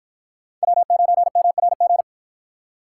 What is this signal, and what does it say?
Morse code, heard as a keyed tone, about 53 wpm, 700 Hz. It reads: W9MRG